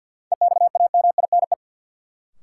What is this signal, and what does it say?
Morse code, heard as a keyed tone, about 52 wpm, 700 Hz. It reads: EXAMINE
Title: Sentences from Top 1600 Words - New Word Drill - 50wpm